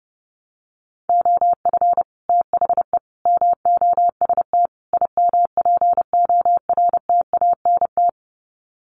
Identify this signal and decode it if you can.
Morse code, heard as a keyed tone, about 30 wpm, 700 Hz. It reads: OF THE MOST IMPORTANT
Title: Sets of 4 Words 30wpm